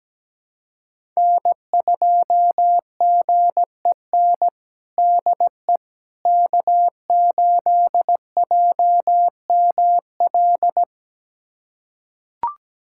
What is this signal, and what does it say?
Morse code, heard as a keyed tone, about 17 wpm, 700 Hz. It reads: N2GEN DE K8JML
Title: USA Callsign DE Callsign - 5 Characters - 17wpm